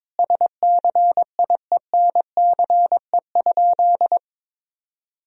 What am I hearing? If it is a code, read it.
Morse code, heard as a keyed tone, about 22 wpm, 700 Hz. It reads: SCIENCE?
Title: Sentences from Top 1000 Words - Review Part 2 - 22wpm